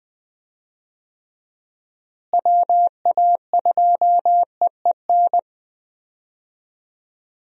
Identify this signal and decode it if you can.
Morse code, heard as a keyed tone, about 21 wpm, 700 Hz. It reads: WA2EEN
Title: USA Call Signs - 6 Characters - Repeated 3x - 20wpm